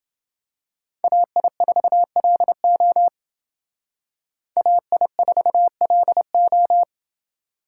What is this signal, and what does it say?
Morse code, heard as a keyed tone, about 28 wpm, 700 Hz. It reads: AI4LO AI4LO